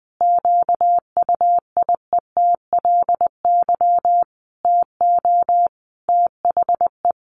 Morse code, heard as a keyed tone, about 20 wpm, 700 Hz. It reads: QUIETLY TO THE
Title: A Sound Of Thunder 20wpm